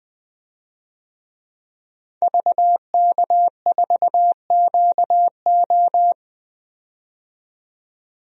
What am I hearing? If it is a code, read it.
Morse code, heard as a keyed tone, about 21 wpm, 700 Hz. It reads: VK4QO
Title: DXCC-100 Call Signs - 5 Characters - 20wpm